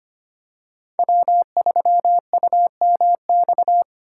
Morse code, heard as a keyed tone, about 25 wpm, 700 Hz. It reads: W3UMX
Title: USA Call Signs - 5 Characters - 2x Character Spacing - 25wpm